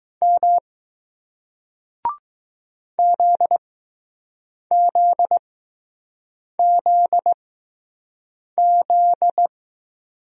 Morse code, heard as a keyed tone, about 20 wpm, spaced as for 6 wpm, 700 Hz. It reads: M ZZZZ